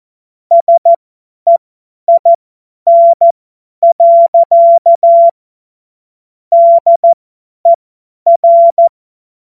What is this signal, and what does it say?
Morse code, heard as a keyed tone, about 14 wpm, spaced as for 10 wpm, 680 Hz. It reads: SEIN. DER
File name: Morsekurs / Witze / Witz 26. April 2022 50 F70